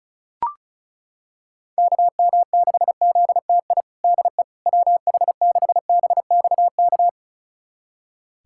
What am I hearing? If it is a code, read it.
Morse code, heard as a keyed tone, about 36 wpm, 700 Hz. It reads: KM6ZTI DE WH6BXK
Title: USA Callsign DE Callsign - 6 Characters - 35wpm